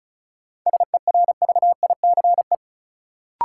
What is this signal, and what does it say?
Morse code, heard as a keyed tone, about 36 wpm, 700 Hz. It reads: SERVICE
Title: Sentences from Top 700 Words - New Word Drill - 35wpm